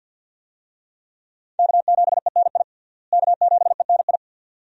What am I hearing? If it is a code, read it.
Morse code, heard as a keyed tone, about 53 wpm, 700 Hz. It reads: K7ENI K7ENI
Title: USA Call Signs - 5 Characters - Repeated 2x - 50wpm